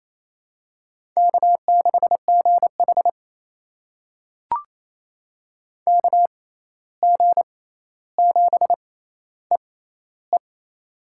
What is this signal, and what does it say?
Morse code, heard as a keyed tone, about 28 wpm, spaced as for 13 wpm, 700 Hz. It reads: K6GH KG7EE
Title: USA Call Signs - 6x Character Spacing - 28wpm